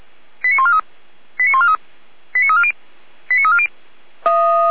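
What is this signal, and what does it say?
Five-tone selective call (ZVEI), heard as quick runs of five tones, tones of about 70 ms. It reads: 89143 89143 89340 89340
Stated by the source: ZVEI five-tone selective call received on an SDR.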